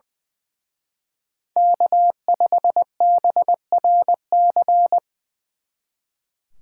Morse code, heard as a keyed tone, about 23 wpm, 700 Hz. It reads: K5BRC
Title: Speed Racing - USA 1x3 Call Signs 20wpm